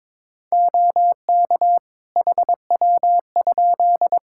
Morse code, heard as a keyed tone, about 21 wpm, 700 Hz. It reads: OK HW?
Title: Practice QSO - Exchange 3 - More Info - 22wpm